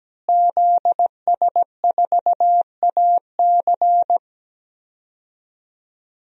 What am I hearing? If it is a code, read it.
Morse code, heard as a keyed tone, about 17 wpm, 700 Hz. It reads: ZS4AC
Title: DXCC-100 Call Signs - 5 Characters - Repeated 2x - 17wpm